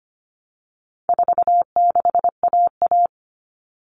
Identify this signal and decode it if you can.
Morse code, heard as a keyed tone, about 25 wpm, 700 Hz. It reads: 46AA
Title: All Cut Numbers - 4-Digit 25wpm